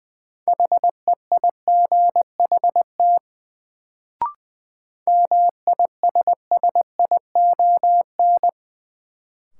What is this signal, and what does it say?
Morse code, heard as a keyed tone, about 20 wpm, 700 Hz. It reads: HEIGHT MISSION
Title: Sentences from Top 1600 Words - New Word Drill - 20wpm